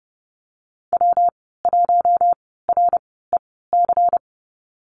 Morse code, heard as a keyed tone, about 30 wpm, spaced as for 17 wpm, 700 Hz. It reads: W1REC
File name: Lesson 19g - Letter w - Call Signs 30wpm